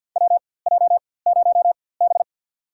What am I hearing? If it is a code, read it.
Morse code, heard as a keyed tone, about 48 wpm, spaced as for 23 wpm, 700 Hz. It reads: WJ0B
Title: USA Call Signs - 4x Character Spacing - 50wpm